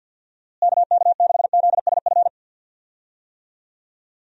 Morse code, heard as a keyed tone, about 51 wpm, 700 Hz. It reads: KK6ZSF
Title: USA Call Signs - 6 Characters - Repeated 2x - 50wpm